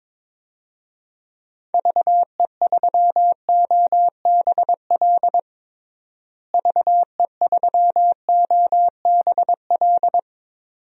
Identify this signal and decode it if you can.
Morse code, heard as a keyed tone, about 22 wpm, 700 Hz. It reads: VE3OBL VE3OBL
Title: DXCC-100 Call Signs - 6 Characters - Repeated 2x - 22wpm